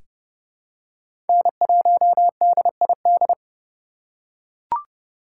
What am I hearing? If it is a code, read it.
Morse code, heard as a keyed tone, about 30 wpm, 700 Hz. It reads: N1DID
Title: K1USN SST QSO Practice - 30wpm